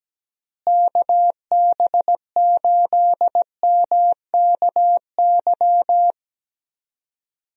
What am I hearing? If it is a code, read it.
Morse code, heard as a keyed tone, about 17 wpm, 700 Hz. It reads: KB8MKY
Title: USA Call Signs - 6 Characters - Repeated 3x - 17wpm